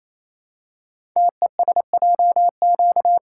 Morse code, heard as a keyed tone, about 27 wpm, 700 Hz. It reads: TESJQ
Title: Code Groups - 5 Letters - 28wpm